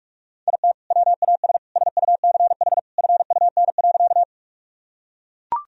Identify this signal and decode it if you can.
Morse code, heard as a keyed tone, about 44 wpm, 700 Hz. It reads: IT WAS SUCH FUN.
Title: Sentences from Top 1000 Words - Review Part 4 - 45wpm